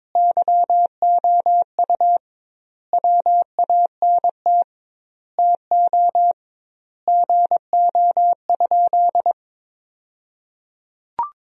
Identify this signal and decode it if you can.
Morse code, heard as a keyed tone, about 22 wpm, 700 Hz. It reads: YOU WANT TO GO?
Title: 2x Word-Spacing - Sentences from Top 300 Words - 22wpm